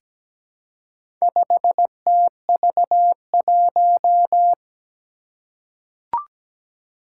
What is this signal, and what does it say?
Morse code, heard as a keyed tone, about 17 wpm, 700 Hz. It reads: 5TV1
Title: Four Challenging Letter-Number Combinations 17wpm